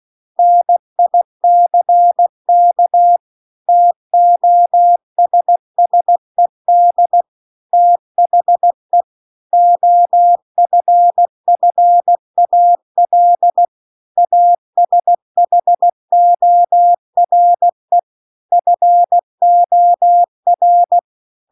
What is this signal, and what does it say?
Morse code, heard as a keyed tone, about 16 wpm, 700 Hz. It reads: NICK TOSSED THE OFFAL ASHORE FOR